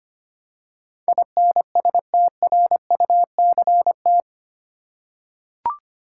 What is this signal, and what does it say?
Morse code, heard as a keyed tone, about 25 wpm, 700 Hz. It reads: INSTRUCT
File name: Eight Letter Words - 25wpm